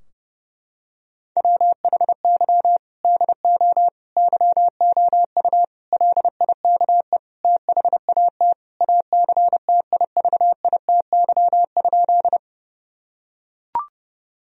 Morse code, heard as a keyed tone, about 30 wpm, 700 Hz. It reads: WHY DO YOU LIKE THAT ACTIVITY?
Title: Sentences from Top 1000 Words - Review Part 4 - 30wpm